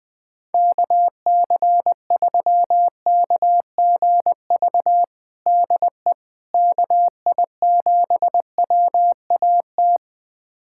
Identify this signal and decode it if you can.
Morse code, heard as a keyed tone, about 21 wpm, 700 Hz. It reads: KC3KGV DE KI7WAT
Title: USA Callsign DE Callsign 20wpm